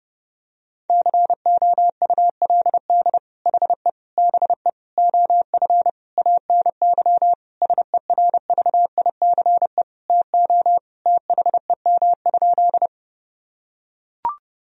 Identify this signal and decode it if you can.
Morse code, heard as a keyed tone, about 30 wpm, 700 Hz. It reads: COULD HE BE OF ANY SERVICE TO THEM?
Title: Sentences from Top 1000 Words - Review Part 4 - 30wpm